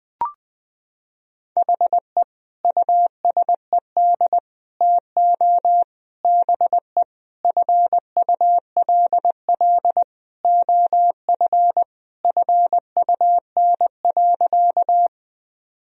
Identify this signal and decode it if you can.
Morse code, heard as a keyed tone, about 21 wpm, 700 Hz. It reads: HE USED TO BE FULL OF FUN.
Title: Sentences from Top 1000 Words - Review Part 4 - 20wpm